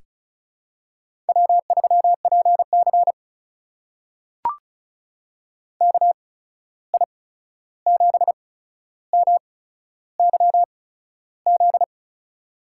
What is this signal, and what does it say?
Morse code, heard as a keyed tone, about 36 wpm, spaced as for 13 wpm, 700 Hz. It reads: W3PC KI7MYZ
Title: USA Call Signs - 8x Character Spacing - 35wpm